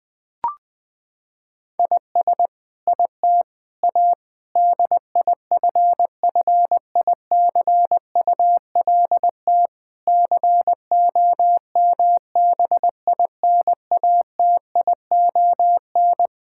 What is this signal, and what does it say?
Morse code, heard as a keyed tone, about 21 wpm, 700 Hz. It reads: IS IT A DIFFICULT COMBINATION
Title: Sentences from Top 1000 Words - Review Part 6 - 20wpm